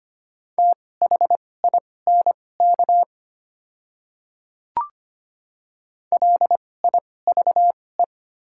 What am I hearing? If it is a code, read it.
Morse code, heard as a keyed tone, about 25 wpm, spaced as for 18 wpm, 700 Hz. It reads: THINK LIVE
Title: 2x Character Spacing - Top 500 Words - 25wpm